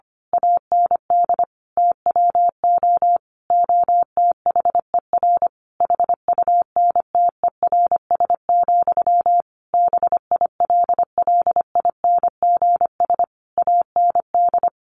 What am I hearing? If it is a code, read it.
Morse code, heard as a keyed tone, about 25 wpm, 700 Hz. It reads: AND TWO OTHER HUNTERS, BILLINGS AND
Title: A Sound Of Thunder 25wpm